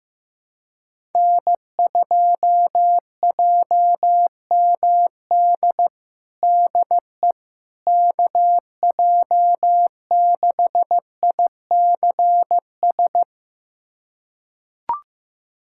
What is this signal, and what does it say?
Morse code, heard as a keyed tone, about 15 wpm, 700 Hz. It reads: N2JMD DE KJ6ICS
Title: USA Callsign DE Callsign 15wpm